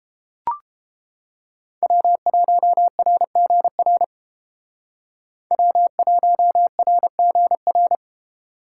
Morse code, heard as a keyed tone, about 30 wpm, 700 Hz. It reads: W1RGR W1RGR